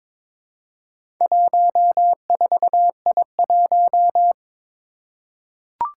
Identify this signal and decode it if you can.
Morse code, heard as a keyed tone, about 23 wpm, 700 Hz. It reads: 14I1
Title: Four Challenging Letter-Number Combinations 22wpm